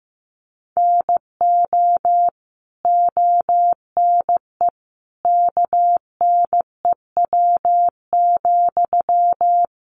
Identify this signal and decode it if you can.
Morse code, heard as a keyed tone, about 15 wpm, 700 Hz. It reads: NO ONE KNEW,
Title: Sentences from Top 1400 Words 15wpm